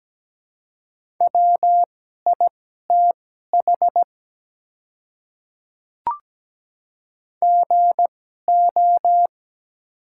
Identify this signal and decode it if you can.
Morse code, heard as a keyed tone, about 17 wpm, spaced as for 12 wpm, 690 Hz. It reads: WITH GO
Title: 2x Character Spacing - Top 100 Words - 17wpm